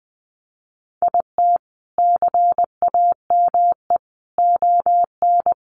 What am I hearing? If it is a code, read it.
Morse code, heard as a keyed tone, about 20 wpm, 700 Hz. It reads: IT CAME ON
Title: A Sound Of Thunder 20wpm